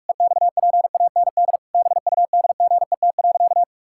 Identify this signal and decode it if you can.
Morse code, heard as a keyed tone, about 46 wpm, 700 Hz. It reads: EXPAND BUDGET.
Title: Pangram Challenge v2 - 45wpm